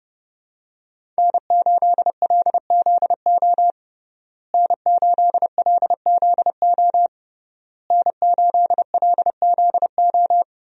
Morse code, heard as a keyed tone, about 30 wpm, 700 Hz. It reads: N8LZO N8LZO N8LZO
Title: USA Call Signs - 5 Characters - Repeated 3x - 30wpm